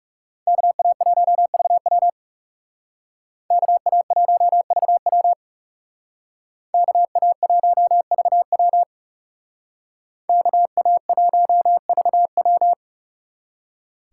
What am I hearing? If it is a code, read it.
Morse code, heard as a keyed tone, about 36 wpm, 700 Hz. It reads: KA1VW KA1VW KA1VW KA1VW
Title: Speed Racing - USA 2x2 Call Signs 30wpm